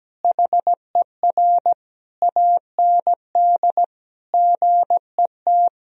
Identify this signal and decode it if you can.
Morse code, heard as a keyed tone, about 17 wpm, 700 Hz. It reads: HER AND GET
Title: Sentences from Top 1000 Words - Review Part 2 - 17wpm